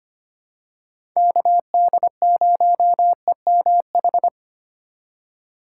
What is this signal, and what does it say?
Morse code, heard as a keyed tone, about 25 wpm, 700 Hz. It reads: KD0EMH
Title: USA Call Signs - 6 Characters - 3x Character Spacing - 25wpm